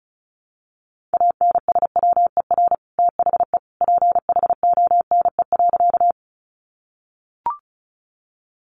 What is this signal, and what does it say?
Morse code, heard as a keyed tone, about 34 wpm, 700 Hz. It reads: ANSWER THE PHONE.
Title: Sentences from Top 1100 Words 35wpm